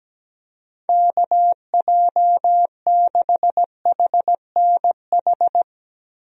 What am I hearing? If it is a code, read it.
Morse code, heard as a keyed tone, about 17 wpm, 700 Hz. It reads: KJ6HNH
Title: USA Call Signs - 6 Characters - Repeated 3x - 17wpm